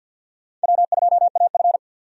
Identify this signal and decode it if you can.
Morse code, heard as a keyed tone, about 51 wpm, 700 Hz. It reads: W2AF
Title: USA Call Signs - 4 Characters - 50wpm